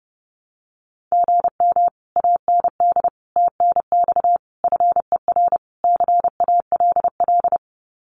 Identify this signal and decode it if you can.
Morse code, heard as a keyed tone, about 30 wpm, 700 Hz. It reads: GM AND TNX FER CALL
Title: Common QSO Phrases Encore 30wpm